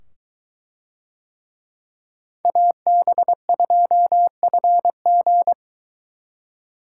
Morse code, heard as a keyed tone, about 21 wpm, 700 Hz. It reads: AB2FG